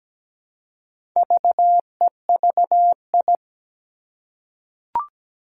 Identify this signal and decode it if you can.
Morse code, heard as a keyed tone, about 17 wpm, 700 Hz. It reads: VEVI
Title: Four Challenging Letter-Number Combinations 17wpm